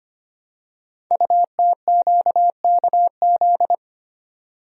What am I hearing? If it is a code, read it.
Morse code, heard as a keyed tone, about 25 wpm, 700 Hz. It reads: UTQKZ
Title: Code Groups - 5 Letters - 25wpm